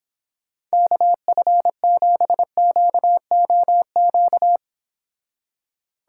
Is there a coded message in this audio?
Yes: Morse code from 0.7 to 4.6 s, heard as a keyed tone, about 26 wpm, 700 Hz. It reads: KF7QOQ